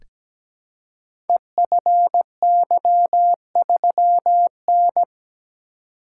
Morse code, heard as a keyed tone, about 17 wpm, 700 Hz. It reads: EFY3N
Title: Code Groups - 5 Letter:Numbers - 17wpm